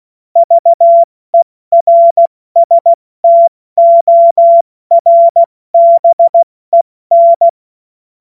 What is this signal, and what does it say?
Morse code, heard as a keyed tone, about 16 wpm, 680 Hz. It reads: VERSTORBEN
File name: Morsekurs / Witze / Witz 17.Oktober 2023 70 F80